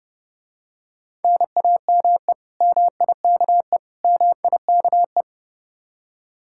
Morse code, heard as a keyed tone, about 30 wpm, 700 Hz. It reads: NAME MIKE MIKE